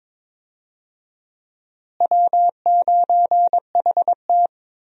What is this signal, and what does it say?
Morse code, heard as a keyed tone, about 23 wpm, 700 Hz. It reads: W9HT
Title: USA Call Signs - 4 Characters - Repeated 3x - 22wpm